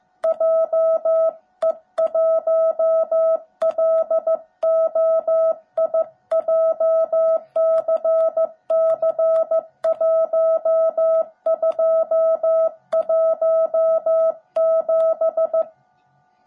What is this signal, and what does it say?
Morse code, heard as a keyed tone, about 14 wpm, 650 Hz. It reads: JE1LOI JCC1217
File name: jh8cbh / 2021cw / cw17b